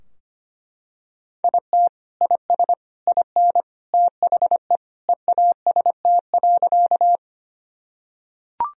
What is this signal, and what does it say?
Morse code, heard as a keyed tone, about 25 wpm, 700 Hz. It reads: IT IS IN THE EAST.
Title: Sentences from Top 1000 Words - Review Part 3 - 25wpm